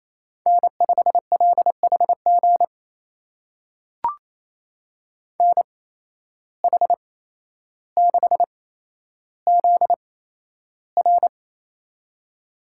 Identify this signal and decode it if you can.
Morse code, heard as a keyed tone, about 28 wpm, spaced as for 12 wpm, 700 Hz. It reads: N5LHG NH6ZR
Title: USA Call Signs - 5 Characters - 8x Character Spacing - 28wpm